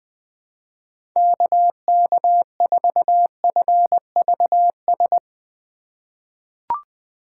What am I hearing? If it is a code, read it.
Morse code, heard as a keyed tone, about 21 wpm, 700 Hz. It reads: KK4FVS
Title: USA Call Signs - 6 Characters - 8x Character Spacing - 20wpm